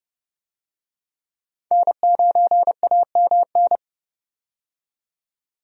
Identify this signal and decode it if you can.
Morse code, heard as a keyed tone, about 31 wpm, 700 Hz. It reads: N9AMN
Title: USA Call Signs - 5 Characters - 30wpm